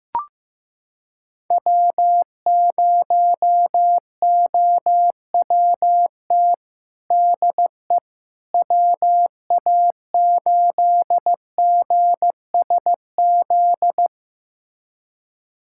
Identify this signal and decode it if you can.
Morse code, heard as a keyed tone, about 15 wpm, 700 Hz. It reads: W0OWT DE WA8GSZ
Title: USA Callsign DE Callsign 15wpm